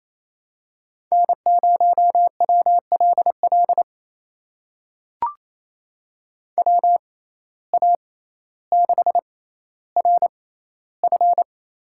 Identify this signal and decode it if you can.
Morse code, heard as a keyed tone, about 28 wpm, spaced as for 14 wpm, 700 Hz. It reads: N0WLL WA6RF
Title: USA Call Signs - 5 Characters - 6x Character Spacing - 28wpm